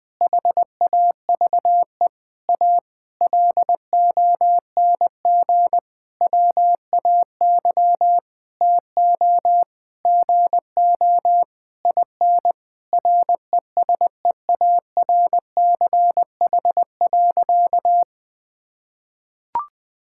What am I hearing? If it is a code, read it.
Morse code, heard as a keyed tone, about 20 wpm, 700 Hz. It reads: HAVE A LONG WAY TO GO IN RESEARCH.
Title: Sentences with 9-Word Phrases - 20wpm